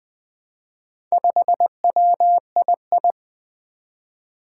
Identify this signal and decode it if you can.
Morse code, heard as a keyed tone, about 20 wpm, 700 Hz. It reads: 5WII
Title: Four Challenging Letter-Number Combinations 20wpm